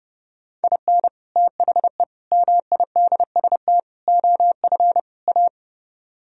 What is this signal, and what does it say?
Morse code, heard as a keyed tone, about 32 wpm, 700 Hz. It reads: IN THE MIDST OF A